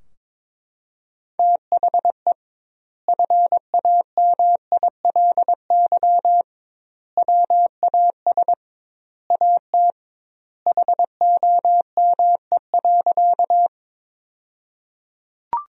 Morse code, heard as a keyed tone, about 21 wpm, 700 Hz. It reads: THE FAMILY WAS AT HOME.
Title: 2x Word-Spacing - Sentences from Top 400 Words - 22wpm